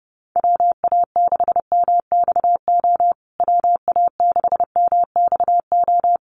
Morse code, heard as a keyed tone, about 28 wpm, 700 Hz. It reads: WA6MXO WA6MXO
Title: USA Call Signs Repeated 2x 30wpm